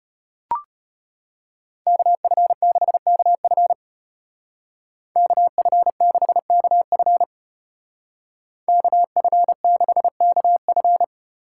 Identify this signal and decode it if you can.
Morse code, heard as a keyed tone, about 33 wpm, 700 Hz. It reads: KF6KF KF6KF KF6KF